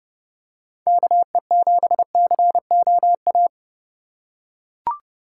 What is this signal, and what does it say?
Morse code, heard as a keyed tone, about 31 wpm, 700 Hz. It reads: KE7COA
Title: USA Call Signs - 6 Characters - 3x Character Spacing - 30wpm